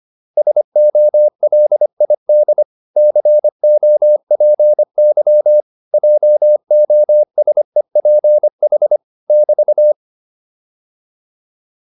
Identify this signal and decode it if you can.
Morse code, heard as a keyed tone, about 25 wpm, 600 Hz. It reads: SOLID COPY JOSEPH <BT>